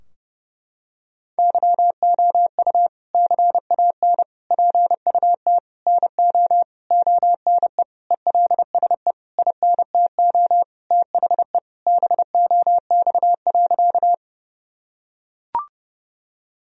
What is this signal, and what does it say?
Morse code, heard as a keyed tone, about 30 wpm, 700 Hz. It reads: YOU CAN PUT NO ONE ELSE INTO THE BOX.
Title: Sentences from Top 1000 Words - Review Part 2 - 30wpm